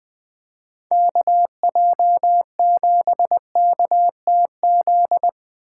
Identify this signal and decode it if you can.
Morse code, heard as a keyed tone, about 20 wpm, 700 Hz. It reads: KJ7KTZ
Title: USA Call Signs - 6 Characters - 4x Character Spacing - 20wpm